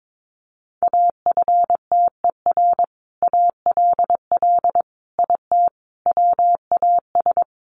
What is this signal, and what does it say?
Morse code, heard as a keyed tone, about 22 wpm, 700 Hz. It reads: AFTER ALL IT WAS
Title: Sentences from Top 1200 Words 22wpm